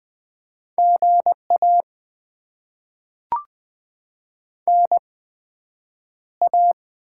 Morse code, heard as a keyed tone, about 20 wpm, spaced as for 9 wpm, 700 Hz. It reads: GA NA
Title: Top 20 QSO Elements - 8x Character Spacing - 20wpm